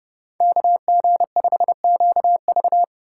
Morse code, heard as a keyed tone, about 28 wpm, 700 Hz. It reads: KG5QV